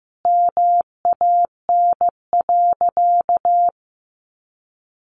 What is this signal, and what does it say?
Morse code, heard as a keyed tone, about 15 wpm, 700 Hz. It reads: MAN.
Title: Sentences from Top 1300 Words 15wpm